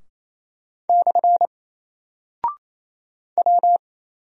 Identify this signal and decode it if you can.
Morse code, heard as a keyed tone, about 28 wpm, 700 Hz. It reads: / W
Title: Single Character (Letter:Number:Punctuation:Prosign) Alt 28wpm